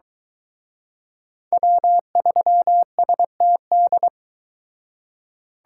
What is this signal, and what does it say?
Morse code, heard as a keyed tone, about 23 wpm, 700 Hz. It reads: W3STD